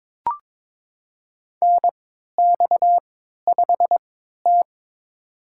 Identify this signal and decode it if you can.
Morse code, heard as a keyed tone, about 22 wpm, spaced as for 12 wpm, 700 Hz. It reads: NX5T
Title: USA Call Signs - 4 Characters - 3x Character Spacing - 22wpm